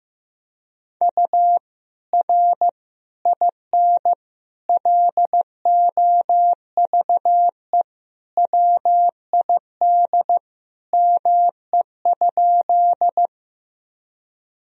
Morse code, heard as a keyed tone, about 15 wpm, 700 Hz. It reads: U R IN LOVE WID ME?
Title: Sentences from Top 400 Words and Abbreviations 15wpm